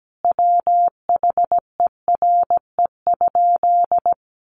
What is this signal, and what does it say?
Morse code, heard as a keyed tone, about 17 wpm, 700 Hz. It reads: WHERE?
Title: Sentences from Top 1200 Words 17wpm